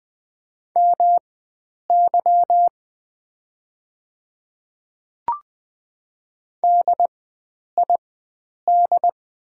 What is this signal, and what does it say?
Morse code, heard as a keyed tone, about 20 wpm, spaced as for 9 wpm, 700 Hz. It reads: MY DID
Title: 4x Character Spacing - Top 100 Words - 20wpm